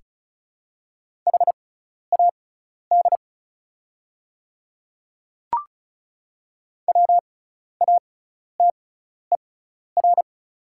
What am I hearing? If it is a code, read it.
Morse code, heard as a keyed tone, about 36 wpm, spaced as for 12 wpm, 690 Hz. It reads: HAD WATER
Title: 6x Character Spacing - Top 500 Words - 35wpm